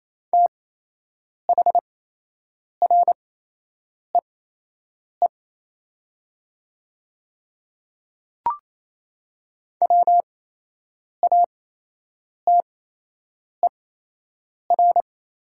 Morse code, heard as a keyed tone, about 28 wpm, spaced as for 8 wpm, 700 Hz. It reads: THREE WATER
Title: 8x Character Spacing - Top 200 Words - 28wpm